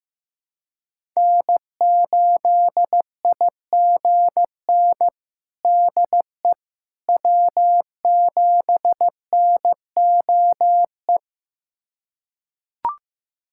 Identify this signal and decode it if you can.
Morse code, heard as a keyed tone, about 16 wpm, 700 Hz. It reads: N8IGN DE W7NOE
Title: USA Callsign DE Callsign - 5 Characters - 15wpm